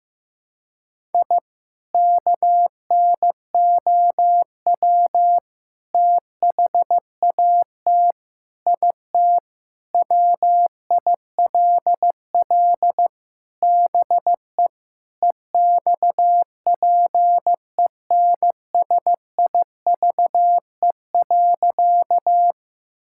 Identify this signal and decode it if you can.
Morse code, heard as a keyed tone, about 15 wpm, 700 Hz. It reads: I KNOW THAT IT WILL BE EXPENSIVE.
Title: Sentences from Top 1000 Words - Review Part 6 - 15wpm